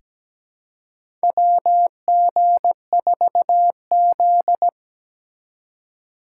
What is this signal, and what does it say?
Morse code, heard as a keyed tone, about 17 wpm, 700 Hz. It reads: WG4Z
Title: USA Call Signs - 4 Characters - 3x Character Spacing - 17wpm